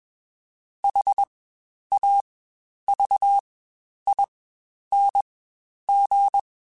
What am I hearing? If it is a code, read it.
Morse code, heard as a keyed tone, about 21 wpm, spaced as for 10 wpm, 800 Hz. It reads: HAVING
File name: Quotes Spaced / Quotes Spaced-mp3s / ChildAsleep sp @20